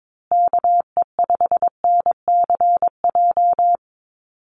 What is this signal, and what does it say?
Morse code, heard as a keyed tone, about 22 wpm, 700 Hz. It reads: KE5NCJ
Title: USA Call Signs Repeated 3x 22wpm